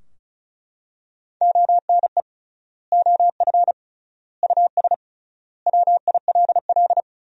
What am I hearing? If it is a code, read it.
Morse code, heard as a keyed tone, about 36 wpm, 700 Hz. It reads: ONE OF US WILL
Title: 3x Word-Spacing - Sentences from Top 300 Words - 35wpm